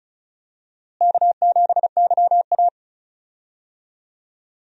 Morse code, heard as a keyed tone, about 36 wpm, 700 Hz. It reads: K7YA
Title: USA Call Signs - 4 Characters - Repeated 2x - 35wpm